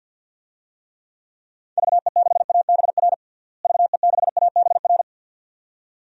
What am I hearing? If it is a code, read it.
Morse code, heard as a keyed tone, about 52 wpm, 700 Hz. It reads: VE6ABR VE6ABR
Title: DXCC-100 Call Signs - 6 Characters - Repeated 2x - 50wpm